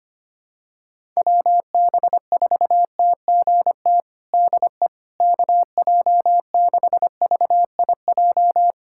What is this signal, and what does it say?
Morse code, heard as a keyed tone, about 25 wpm, 700 Hz. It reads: WB4TGT DE KJ6VIJ
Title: USA Callsign DE Callsign - 6 Characters - 25wpm